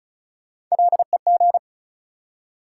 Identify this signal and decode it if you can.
Morse code, heard as a keyed tone, about 36 wpm, 700 Hz. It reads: LEG
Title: Three Letter Words - 35wpm